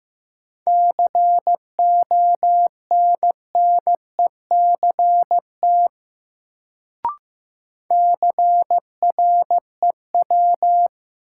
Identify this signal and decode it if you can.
Morse code, heard as a keyed tone, about 15 wpm, 700 Hz. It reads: CONNECT CREW
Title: Sentences from Top 1500 Words - New Word Drill - 15wpm